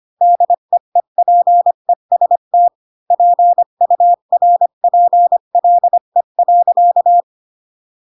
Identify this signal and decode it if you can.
Morse code, heard as a keyed tone, about 25 wpm, 700 Hz. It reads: DEEPEST PURPLE.
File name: Pinocchio / Pinocchio ch1 20 25 ik1hge